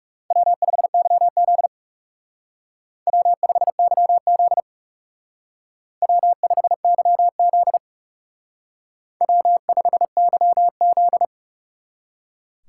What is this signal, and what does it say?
Morse code, heard as a keyed tone, about 36 wpm, 700 Hz. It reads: W5YZ W5YZ W5YZ W5YZ
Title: Speed Racing - USA 1x2 Call Signs 30wpm